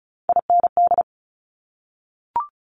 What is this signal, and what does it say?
Morse code, heard as a keyed tone, about 36 wpm, 700 Hz. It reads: IND
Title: Top Three Letter Combinations in Words 35wpm